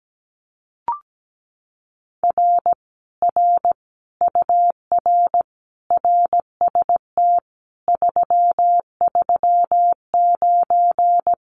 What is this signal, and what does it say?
Morse code, heard as a keyed tone, about 17 wpm, 700 Hz. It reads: R R UR RST 339
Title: Common QSO Phrases Encore 17wpm